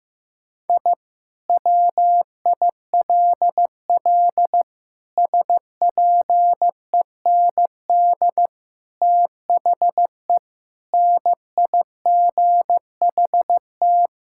Morse code, heard as a keyed tone, about 16 wpm, 700 Hz. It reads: I WILL SPEND THE NIGHT
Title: Sentences from Top 1000 Words - Review Part 5 - 15wpm